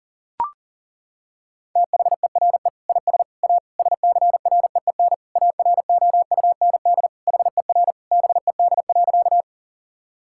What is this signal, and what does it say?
Morse code, heard as a keyed tone, about 40 wpm, 700 Hz. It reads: THERE IS A SCREEN AROUND HER BED.
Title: Sentences from Top 1000 Words - Review Part 6 - 40wpm